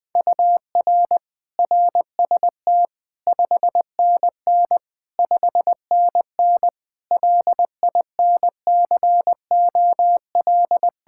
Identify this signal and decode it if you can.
Morse code, heard as a keyed tone, about 21 wpm, 700 Hz. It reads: UR RST 5NN 5NN LINCOL
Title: Practice QSO - Exchange 1 - Beginning - 20wpm